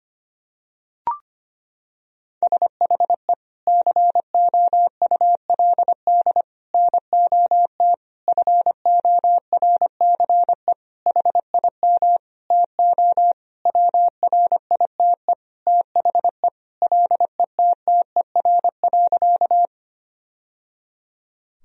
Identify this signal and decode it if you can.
Morse code, heard as a keyed tone, about 25 wpm, 700 Hz. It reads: SHE COULD NOT FORCE HIM TO WRITE THE LETTER.
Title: Sentences from Top 1000 Words - Review Part 3 - 25wpm